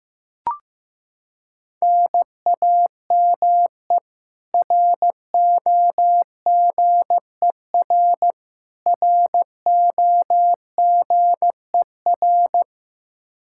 Morse code, heard as a keyed tone, about 15 wpm, 700 Hz. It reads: NAME ROGER ROGER
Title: Practice QSO - Exchange 1 - Beginning - 15wpm